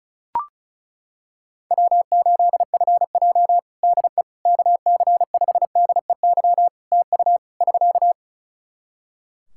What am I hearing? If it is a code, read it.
Morse code, heard as a keyed tone, about 36 wpm, 700 Hz. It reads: W8FJ DE KC5DEY TU <SK>
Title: Practice QSO - Exchange 4 - Ending - 35wpm